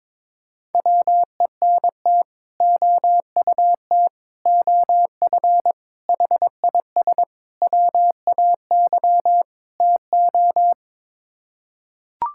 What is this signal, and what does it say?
Morse code, heard as a keyed tone, about 22 wpm, 700 Hz. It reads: WENT OUT OF HIS WAY TO
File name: practice / 22wpm / Sets of 6 Words - 22wpm